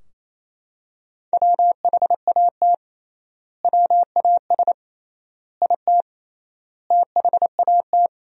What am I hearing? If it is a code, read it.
Morse code, heard as a keyed tone, about 28 wpm, 700 Hz. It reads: WHAT WAS IT THAT
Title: 3x Word-Spacing - Sentences from Top 300 Words - 28wpm